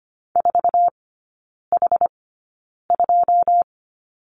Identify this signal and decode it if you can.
Morse code, heard as a keyed tone, about 25 wpm, spaced as for 9 wpm, 700 Hz. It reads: 4H2
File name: course / 25:10wpm / 093 Lesson 16e - Letter c - Three Characters 25wpm